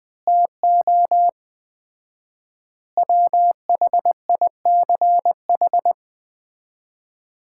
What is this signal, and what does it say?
Morse code, heard as a keyed tone, about 20 wpm, 700 Hz. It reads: TO WHICH
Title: 4x Word-Spacing - Sets of 4 Words - 20wpm